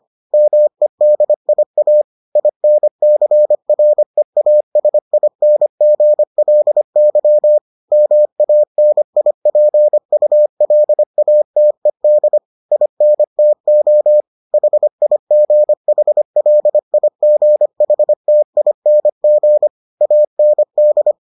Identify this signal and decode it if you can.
Morse code, heard as a keyed tone, about 25 wpm, 600 Hz. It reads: MEDIA INCREASINGLY MANIPULATED INTO HIGHLIGHTING AND